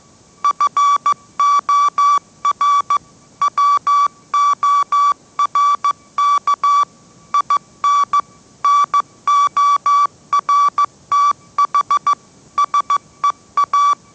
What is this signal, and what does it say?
Morse code, heard as a keyed tone, about 15 wpm, 1150 Hz. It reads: FOR WORK IN NORTH SEA
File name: mrx / mrx065 16wpm